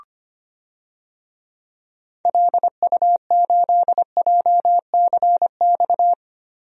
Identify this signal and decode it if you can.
Morse code, heard as a keyed tone, about 25 wpm, 700 Hz. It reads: LU8JCX
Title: DXCC-100 Call Signs - 6 Characters - Repeated 3x - 25wpm